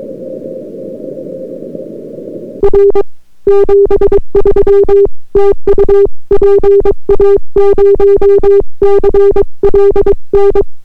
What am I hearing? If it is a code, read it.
Morse code, heard as a keyed tone, about 22 wpm, 400 Hz. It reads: R 73 TU PA0CLN